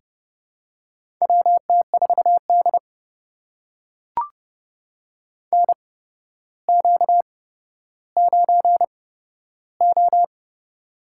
Morse code, heard as a keyed tone, about 30 wpm, spaced as for 13 wpm, 700 Hz. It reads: WT4D NQ9O
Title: USA Call Signs - 4 Characters - 8x Character Spacing - 30wpm